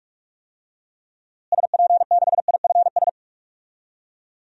Morse code, heard as a keyed tone, about 41 wpm, 700 Hz. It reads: SP6IFS
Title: DXCC-100 Call Signs - 6 Characters - Repeated 2x - 45wpm